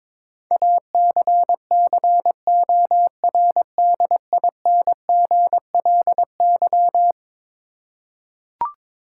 Morse code, heard as a keyed tone, about 22 wpm, 700 Hz. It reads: ACCORDINGLY
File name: practice / 22wpm / Eleven Letter Words - 22wpm